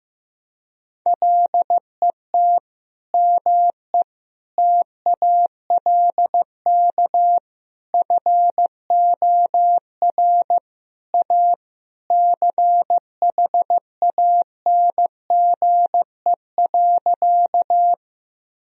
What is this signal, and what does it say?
Morse code, heard as a keyed tone, about 15 wpm, 700 Hz. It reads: LET ME TALK FOR A CHANGE.
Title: Sentences from Top 1000 Words - Review Part 1 - 15wpm